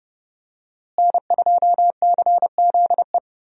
Morse code, heard as a keyed tone, about 30 wpm, 700 Hz. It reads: N2CZE
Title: USA Call Signs - 5 Characters - Repeated 2x - 30wpm